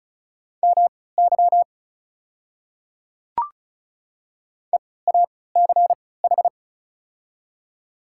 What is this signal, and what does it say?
Morse code, heard as a keyed tone, about 36 wpm, spaced as for 20 wpm, 690 Hz. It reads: MY EACH
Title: 3x Character Spacing - Top 100 Words - 35wpm